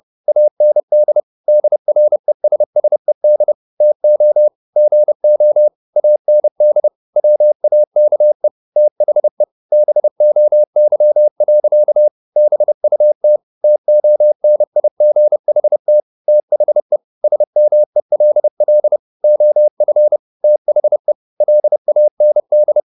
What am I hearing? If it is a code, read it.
Morse code, heard as a keyed tone, about 30 wpm, 600 Hz. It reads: AND DRESSED TO GO AND WAKE THE BOY. BUT TONIGHT THE SMELL OF THE LAND
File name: the old man and the sea / the old man and the sea 1 30wpm